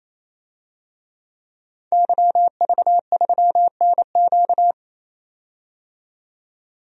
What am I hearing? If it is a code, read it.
Morse code, heard as a keyed tone, about 28 wpm, 700 Hz. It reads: YV3NQ
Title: DXCC-100 Call Signs - 5 Characters - 28wpm